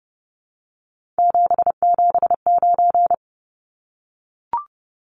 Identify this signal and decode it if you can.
Morse code, heard as a keyed tone, about 30 wpm, 700 Hz. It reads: 779
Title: Numbers - 3-Digit 30wpm